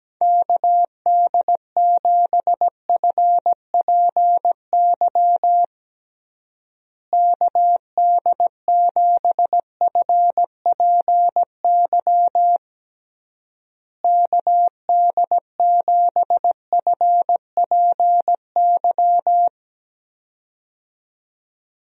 Morse code, heard as a keyed tone, about 17 wpm, 700 Hz. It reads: KD7FPY KD7FPY KD7FPY
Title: USA Call Signs - 6 Characters - Repeated 3x - 17wpm